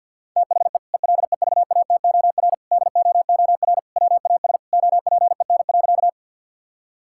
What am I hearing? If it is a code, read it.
Morse code, heard as a keyed tone, about 50 wpm, 700 Hz. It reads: THE ELEVATOR DOOR WAS OPEN.
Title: Sentences from Top 1600 Words 50wpm